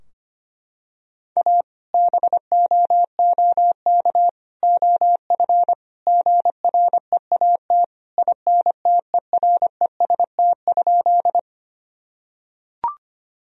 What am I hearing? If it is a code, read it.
Morse code, heard as a keyed tone, about 25 wpm, 700 Hz. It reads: A BOOK OF GREAT INTEREST?
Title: Sentences from Top 1000 Words - Review Part 2 - 25wpm